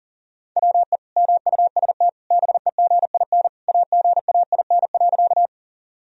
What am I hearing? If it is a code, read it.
Morse code, heard as a keyed tone, about 40 wpm, 700 Hz. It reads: WE MUST BEGIN AGAIN.
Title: Sentences from Top 1000 Words - Review Part 1 - 40wpm